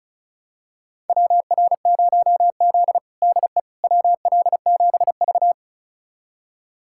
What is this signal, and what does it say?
Morse code, heard as a keyed tone, about 36 wpm, 700 Hz. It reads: WR0Z DE WL7V
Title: USA Callsign DE Callsign - 4 Characters - 35wpm